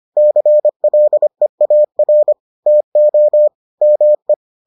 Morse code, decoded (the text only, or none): CLEAR TO ME